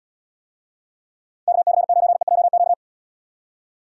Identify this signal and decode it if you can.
Morse code, heard as a keyed tone, about 73 wpm, 700 Hz. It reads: KC9EJQ